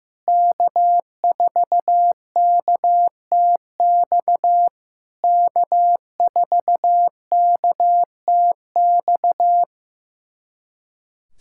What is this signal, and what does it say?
Morse code, heard as a keyed tone, about 15 wpm, 700 Hz. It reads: K4KTX K4KTX
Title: SOTA QSO Practice 15wpm